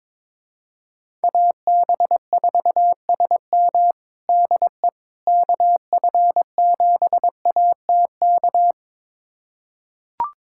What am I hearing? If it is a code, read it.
Morse code, heard as a keyed tone, about 22 wpm, 700 Hz. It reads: AB4SM DE KF7ATK
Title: USA Callsign DE Callsign 22wpm